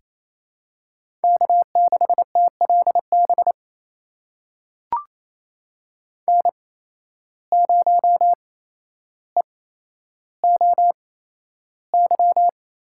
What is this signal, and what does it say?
Morse code, heard as a keyed tone, about 28 wpm, spaced as for 12 wpm, 700 Hz. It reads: K6TLB N0EOY
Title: USA Call Signs - 5 Characters - 8x Character Spacing - 28wpm